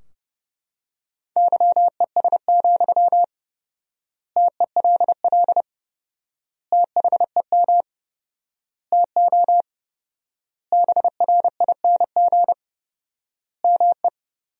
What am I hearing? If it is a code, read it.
Morse code, heard as a keyed tone, about 30 wpm, 700 Hz. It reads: YES, TELL THEM TO BRING ME
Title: 4x Word-Spacing - Sentences from Top 500 Words - 30wpm